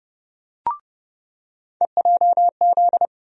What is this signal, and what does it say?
Morse code, heard as a keyed tone, about 30 wpm, 700 Hz. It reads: EJZ
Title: USA Call Sign Postfixes - 3 Letters 30wpm